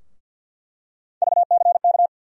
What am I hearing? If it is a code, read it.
Morse code, heard as a keyed tone, about 47 wpm, 700 Hz. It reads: VCK
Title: USA Call Sign Postfixes - 3 Letters 50wpm